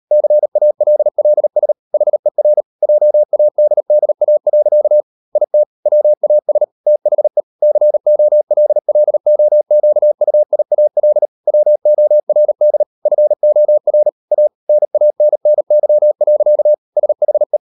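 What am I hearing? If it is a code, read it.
Morse code, heard as a keyed tone, about 38 wpm, 600 Hz. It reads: CALLS HER JADDA. IT WAS THE COLLOQUIAL WORD FOR A NANNY. SHE
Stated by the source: Morse code practice at 38 WPM, sending Ken Follett's novel Never in English, from CW med Gnister.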